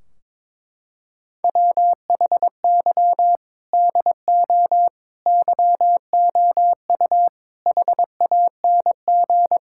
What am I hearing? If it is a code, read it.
Morse code, heard as a keyed tone, about 22 wpm, 700 Hz. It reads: WHY DO YOU HANG
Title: Sentences from Top 1600 Words 22wpm